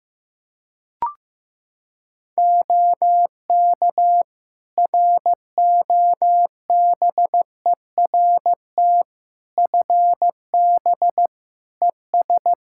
Morse code, heard as a keyed tone, about 16 wpm, 700 Hz. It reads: OK ROBERT FB ES
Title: K1USN SST QSO Practice 15wpm